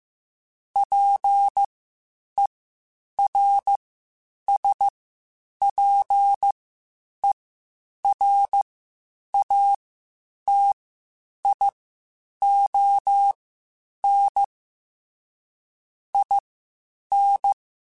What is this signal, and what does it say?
Morse code, heard as a keyed tone, about 15 wpm, spaced as for 8 wpm, 800 Hz. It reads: PERSPERATION IN
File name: Quotes Spaced / Quotes Spaced-mp3s / Persperation sp @13